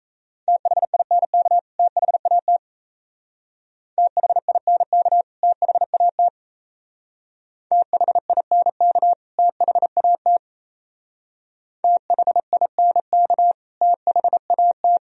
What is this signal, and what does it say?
Morse code, heard as a keyed tone, about 33 wpm, 690 Hz. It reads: THINK THAT THINK THAT THINK THAT THINK THAT
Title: Speed-Racing - Sets of 2 Words 28wpm